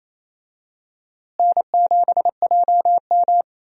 Morse code, heard as a keyed tone, about 28 wpm, 700 Hz. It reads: N7JM
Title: USA Call Signs - 4 Characters - Repeated 3x - 28wpm